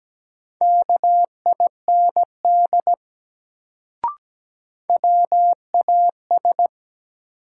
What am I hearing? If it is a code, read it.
Morse code, heard as a keyed tone, about 17 wpm, 700 Hz. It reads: KIND WAS
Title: Sets of 2 Words - New Word Drill - 17wpm